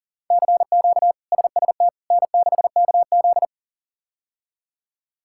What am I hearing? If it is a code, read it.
Morse code, heard as a keyed tone, about 40 wpm, 700 Hz. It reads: CQ SST N6KZ
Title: K1USN SST QSO Practice - 40wpm